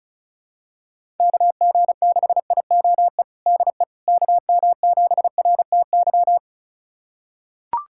Morse code, heard as a keyed tone, about 36 wpm, 700 Hz. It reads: KG6IOE DE KM7RTY
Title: USA Callsign DE Callsign 35wpm